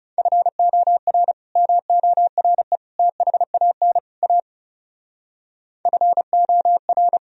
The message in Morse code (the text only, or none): FOR MORE THAN A FOR